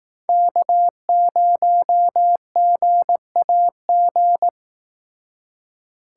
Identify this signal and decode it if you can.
Morse code, heard as a keyed tone, about 19 wpm, 700 Hz. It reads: K0GAG